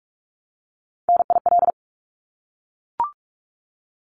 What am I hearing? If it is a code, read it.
Morse code, heard as a keyed tone, about 44 wpm, 700 Hz. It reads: NIL
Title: Abbreviations 45wpm